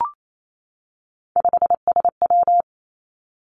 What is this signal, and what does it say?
Morse code, heard as a keyed tone, about 28 wpm, 700 Hz. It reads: 5SW
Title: Three Challenging Letter-Number Combinations 28wpm